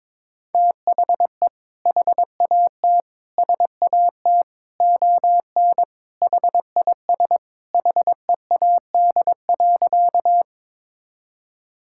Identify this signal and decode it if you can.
Morse code, heard as a keyed tone, about 21 wpm, 700 Hz. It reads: THE HAT SAT ON HIS HEAD.
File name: practice / 22wpm / Sentences with 2-Word Phrases - 22wpm